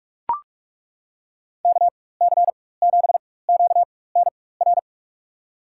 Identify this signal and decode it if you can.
Morse code, heard as a keyed tone, about 43 wpm, spaced as for 21 wpm, 700 Hz. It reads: KC7QNR
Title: USA Call Signs - 6 Characters - 4x Character Spacing - 45wpm